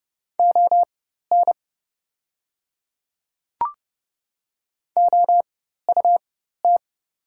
Morse code, heard as a keyed tone, about 30 wpm, spaced as for 14 wpm, 700 Hz. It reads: ON OUT
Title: 4x Character Spacing - Top 100 Words - 30wpm